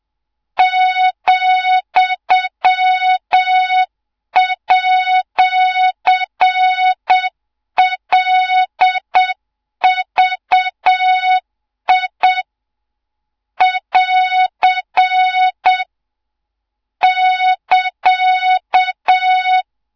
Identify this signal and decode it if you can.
Morse code, heard as a keyed tone, about 7 wpm, 3700 Hz, with noise. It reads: ,@LVI <AR> <KA>
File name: Morze-HA2MN / hang / 24 GYAKORLAT